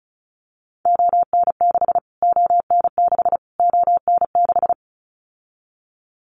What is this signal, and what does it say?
Morse code, heard as a keyed tone, about 34 wpm, 700 Hz. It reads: ON6 ON6 ON6
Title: Callsign Prefixes for Top 25 DXCC - Repeated 3x - 35wpm